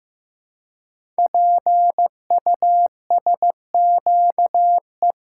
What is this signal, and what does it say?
Morse code, heard as a keyed tone, about 15 wpm, 700 Hz. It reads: PUSQE
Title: Code Groups - 5 Letters - 15wpm